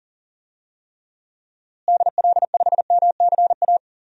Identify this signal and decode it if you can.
Morse code, heard as a keyed tone, about 37 wpm, 700 Hz. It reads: DL5MCA
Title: DXCC-100 Call Signs - 6 Characters - Repeated 2x - 40wpm